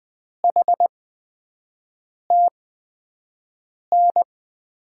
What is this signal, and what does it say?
Morse code, heard as a keyed tone, about 20 wpm, spaced as for 5 wpm, 700 Hz. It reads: HTN
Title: USA Call Signs - 6 Characters - 8x Character Spacing - 20wpm